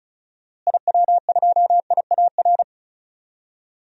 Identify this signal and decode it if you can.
Morse code, heard as a keyed tone, about 36 wpm, 700 Hz. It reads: IW2IAR
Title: DXCC-100 Call Signs - 6 Characters - Repeated 2x - 35wpm